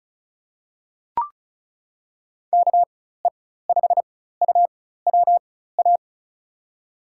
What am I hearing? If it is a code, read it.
Morse code, heard as a keyed tone, about 34 wpm, spaced as for 16 wpm, 700 Hz. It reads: KE5UWA
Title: USA Call Signs - 6 Characters - 4x Character Spacing - 35wpm